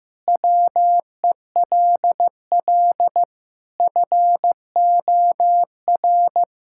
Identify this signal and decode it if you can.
Morse code, heard as a keyed tone, about 15 wpm, 700 Hz. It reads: WELL FOR
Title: Sentences from Top 1000 Words - Review Part 6 - 15wpm